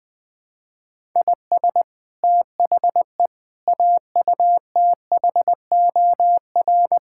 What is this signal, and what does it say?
Morse code, heard as a keyed tone, about 20 wpm, 700 Hz. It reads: IS THE AUTHOR
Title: Build A Phrase - 4-Word Phrases - 20wpm